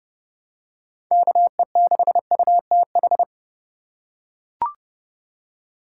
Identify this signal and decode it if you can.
Morse code, heard as a keyed tone, about 32 wpm, 700 Hz. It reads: KE6UTH
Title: USA Call Signs - 6 Characters - 6x Character Spacing - 30wpm